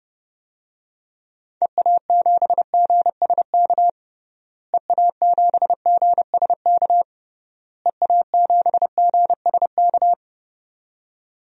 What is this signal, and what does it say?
Morse code, heard as a keyed tone, about 30 wpm, 700 Hz. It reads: EA7GSK EA7GSK EA7GSK
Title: DXCC-100 Call Signs - 6 Characters - Repeated 3x - 30wpm